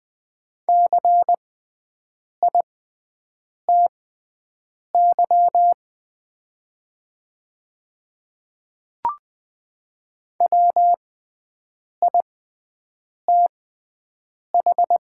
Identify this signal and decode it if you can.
Morse code, heard as a keyed tone, about 20 wpm, spaced as for 7 wpm, 700 Hz. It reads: CITY WITH
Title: 6x Character Spacing - Top 300 Words - 20wpm